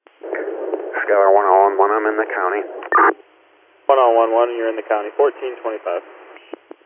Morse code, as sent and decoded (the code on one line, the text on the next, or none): none